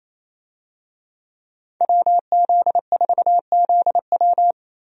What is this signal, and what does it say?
Morse code, heard as a keyed tone, about 28 wpm, 700 Hz. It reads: WZ4ZW
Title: USA Call Signs - 5 Characters - 28wpm